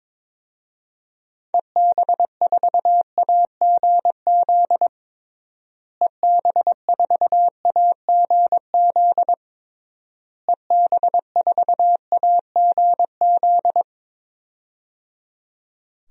Morse code, heard as a keyed tone, about 22 wpm, 700 Hz. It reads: EB4AGZ EB4AGZ EB4AGZ
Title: DXCC-100 Call Signs - 6 Characters - Repeated 3x - 22wpm